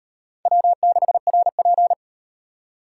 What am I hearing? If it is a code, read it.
Morse code, heard as a keyed tone, about 35 wpm, 700 Hz. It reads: W6RP